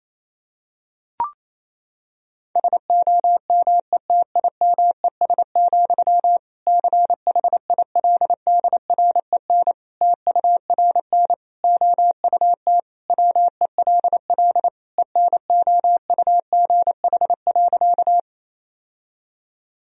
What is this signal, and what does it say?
Morse code, heard as a keyed tone, about 27 wpm, 700 Hz. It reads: SOMETIMES, CHILDREN TURN OUT WELL ENOUGH.
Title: Sentences from Top 1000 Words - Review Part 3 - 28wpm